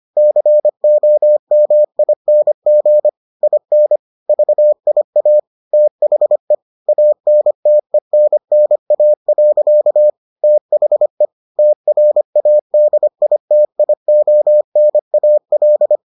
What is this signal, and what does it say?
Morse code, heard as a keyed tone, about 25 wpm, 600 Hz. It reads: COMING IN VIA THE ANTENNA. THE TRADITIONAL